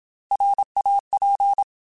Morse code, heard as a keyed tone, about 27 wpm, 800 Hz. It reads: RAP
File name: Alphabetical / Alphabetical mp3s / Rs @25